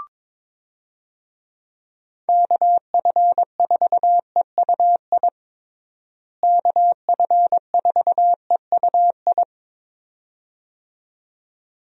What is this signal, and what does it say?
Morse code, heard as a keyed tone, about 22 wpm, 700 Hz. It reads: KF4EUI KF4EUI
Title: USA Call Signs - 6 Characters - Repeated 2x - 22wpm